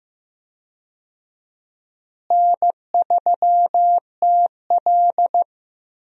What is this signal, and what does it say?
Morse code, heard as a keyed tone, about 15 wpm, 700 Hz. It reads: N3TL
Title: USA Call Signs - 4 Characters - Repeated 2x - 15wpm